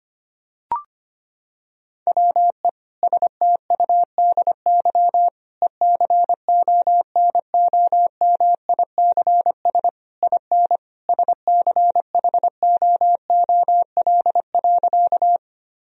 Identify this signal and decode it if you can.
Morse code, heard as a keyed tone, about 25 wpm, 700 Hz. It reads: WE STUDY ECONOMICS IN SCHOOL.
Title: Sentences from Top 1000 Words - Review Part 3 - 25wpm